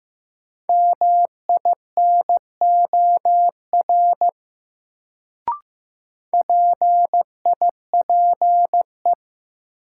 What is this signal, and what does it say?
Morse code, heard as a keyed tone, about 15 wpm, 700 Hz. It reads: MINOR PIPE
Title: Sentences from Top 1500 Words - New Word Drill - 15wpm